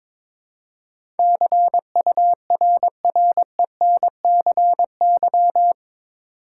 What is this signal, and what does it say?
Morse code, heard as a keyed tone, about 22 wpm, 700 Hz. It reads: CURRENCY
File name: practice / 22wpm / Eight Letter Words - 22wpm